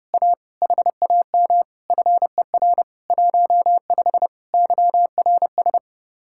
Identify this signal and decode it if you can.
Morse code, heard as a keyed tone, about 30 wpm, 700 Hz. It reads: A HAM FER 15 YRS